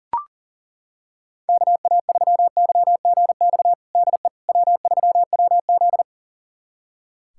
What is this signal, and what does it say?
Morse code, heard as a keyed tone, about 37 wpm, 700 Hz. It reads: KA3YGX DE W3WZ
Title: USA Callsign DE Callsign 40wpm